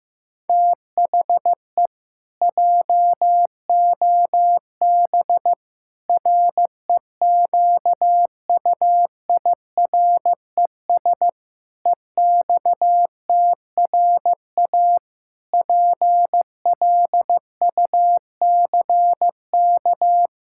Morse code, heard as a keyed tone, about 16 wpm, 700 Hz. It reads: THE JOB REQUIRES EXTRA PLUCK
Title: Pangram Challenge v2 - 15wpm